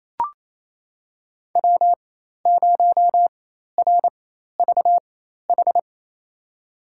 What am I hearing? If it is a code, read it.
Morse code, heard as a keyed tone, about 28 wpm, spaced as for 13 wpm, 700 Hz. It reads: W0RVH
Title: USA Call Signs - 5 Characters - 4x Character Spacing - 28wpm